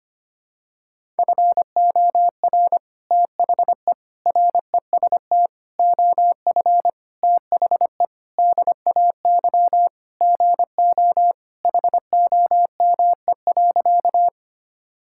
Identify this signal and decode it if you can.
Morse code, heard as a keyed tone, about 25 wpm, 700 Hz. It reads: FOR THE REST OF THE DAY GO HOME.
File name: practice / 25wpm / Sentences with 3-Word Phrases - 25wpm